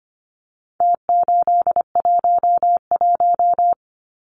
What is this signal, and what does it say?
Morse code, heard as a keyed tone, about 25 wpm, 700 Hz. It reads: T811
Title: Numbers - Contest Style - 4-Digit 25wpm